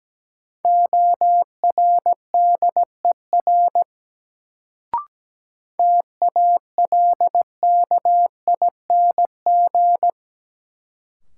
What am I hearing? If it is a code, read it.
Morse code, heard as a keyed tone, about 17 wpm, 700 Hz. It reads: ORDER TALKING
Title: Sets of 2 Words - New Word Drill - 17wpm